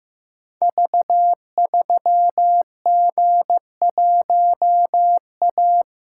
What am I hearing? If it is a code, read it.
Morse code, heard as a keyed tone, about 15 wpm, 700 Hz. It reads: V3G1A
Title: Code Groups - 5 Letter:Numbers - 15wpm